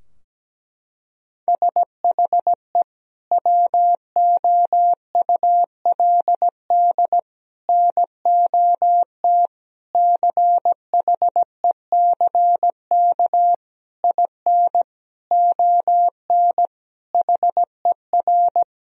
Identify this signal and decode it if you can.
Morse code, heard as a keyed tone, about 17 wpm, 700 Hz. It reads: SHE WOULD NOT CHECK IN ON HER
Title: Sentences from Top 1000 Words - Review Part 3 - 17wpm